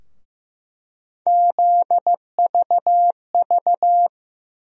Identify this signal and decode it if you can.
Morse code, heard as a keyed tone, about 15 wpm, 700 Hz. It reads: ZVV
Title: USA Call Sign Postfixes - 3 Letters 15wpm